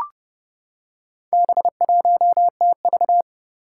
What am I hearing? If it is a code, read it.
Morse code, heard as a keyed tone, about 30 wpm, 700 Hz. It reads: B1TV
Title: Four Challenging Letter-Number Combinations 30wpm